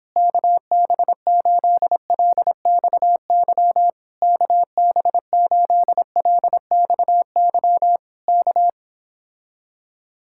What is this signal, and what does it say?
Morse code, heard as a keyed tone, about 26 wpm, 700 Hz. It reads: KB8LXY KB8LXY K